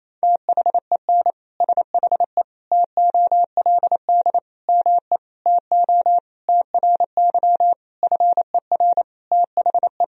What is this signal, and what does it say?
Morse code, heard as a keyed tone, about 28 wpm, 700 Hz. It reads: THEN SHE TOLD ME TO TRY FER THE
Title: Sentences from Top 400 Words and Abbreviations 28wpm